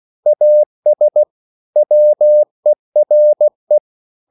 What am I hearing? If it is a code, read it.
Morse code, heard as a keyed tone, about 16 wpm, 600 Hz. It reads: AS WERE